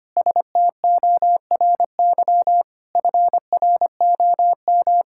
Morse code, heard as a keyed tone, about 25 wpm, 700 Hz. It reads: STORY FROM
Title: Sentences from Top 1000 Words - Review Part 5 - 25wpm